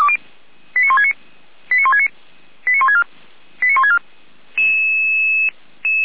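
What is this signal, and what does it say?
Five-tone selective call (ZVEI), heard as quick runs of five tones, tones of about 80 ms. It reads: 89169 89169 89164 89164
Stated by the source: ZVEI five-tone selective call for address 89249, received by SDR, from the feuerwehr-orlamuende (Orlamünde fire brigade) audio server.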